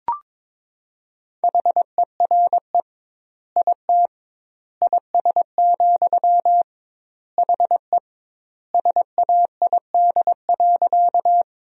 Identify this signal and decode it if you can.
Morse code, heard as a keyed tone, about 22 wpm, 700 Hz. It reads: HERE IT IS, HE SAID.
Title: 2x Word-Spacing - Sentences from Top 200 Words - 22wpm